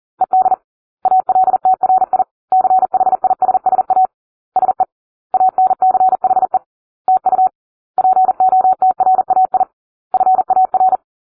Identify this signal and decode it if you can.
Morse code, heard as a keyed tone, about 40 wpm, 750 Hz. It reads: ED ALTRI CHISSA SE ANCHE TU POTRAI FAR